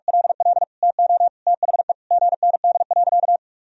Morse code, heard as a keyed tone, about 46 wpm, 700 Hz. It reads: LL TO THE GND.
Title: Sentences from Top 500 Words and Abbreviations 45wpm